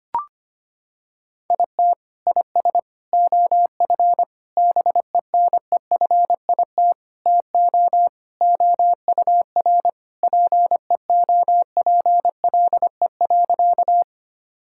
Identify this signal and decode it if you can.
Morse code, heard as a keyed tone, about 25 wpm, 700 Hz. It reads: IT IS OF BENEFIT TO OUR PEOPLE.
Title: Sentences from Top 1000 Words - Review Part 6 - 25wpm